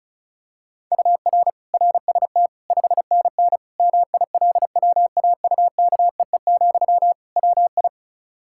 Morse code, heard as a keyed tone, about 36 wpm, 700 Hz. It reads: UR RST 5NN MILWAUKEE, WI
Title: K1USN SST QSO Practice 35wpm